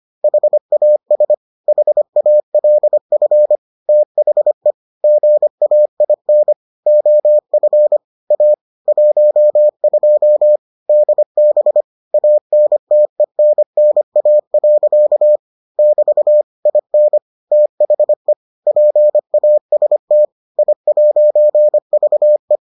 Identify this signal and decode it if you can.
Morse code, heard as a keyed tone, about 25 wpm, 600 Hz. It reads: HAS HALF THE GAIN OF A 12 DB ANTENNA. <BT> IN THE PAST I'VE